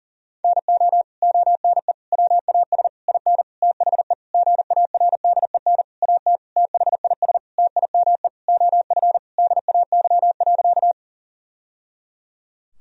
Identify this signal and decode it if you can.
Morse code, heard as a keyed tone, about 37 wpm, 700 Hz. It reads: NO ONE WAS IN THE GARDEN AT THIS TIME OF DAY.
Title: Sentences from Top 1000 Words - Review Part 4 - 40wpm